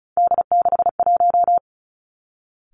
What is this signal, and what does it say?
Morse code, heard as a keyed tone, about 36 wpm, 700 Hz. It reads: D61
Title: Three Challenging Letter-Number Combinations 35wpm